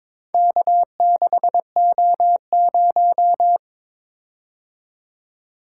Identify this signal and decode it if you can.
Morse code, heard as a keyed tone, about 22 wpm, 700 Hz. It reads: K6O0
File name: practice / 22wpm / Four Letter-Number Combinations - 22wpm